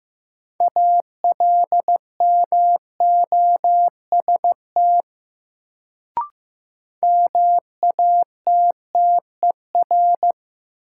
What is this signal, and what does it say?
Morse code, heard as a keyed tone, about 15 wpm, 700 Hz. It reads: ALMOST MATTER
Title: Sentences from Top 700 Words - New Word Drill - 15wpm